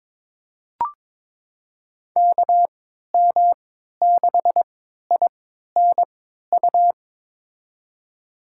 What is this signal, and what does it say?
Morse code, heard as a keyed tone, about 22 wpm, spaced as for 12 wpm, 700 Hz. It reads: KM6INU
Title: USA Call Signs - 6 Characters - 3x Character Spacing - 22wpm